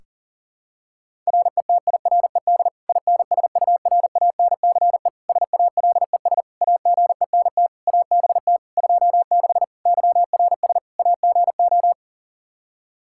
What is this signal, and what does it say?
Morse code, heard as a keyed tone, about 37 wpm, 700 Hz. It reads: RETIRED INSURANCE SALES AGENT ABT 26 YRS AGO
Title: Practice QSO - Exchange 3 - More Info - 40wpm